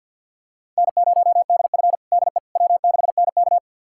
Morse code, heard as a keyed tone, about 47 wpm, 700 Hz. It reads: N0DF DE W6NK
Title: USA Callsign DE Callsign - 4 Characters - 50wpm